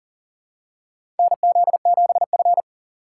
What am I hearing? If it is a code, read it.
Morse code, heard as a keyed tone, about 40 wpm, 700 Hz. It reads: NZ7F
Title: USA Call Signs - 3x Character Spacing - 40wpm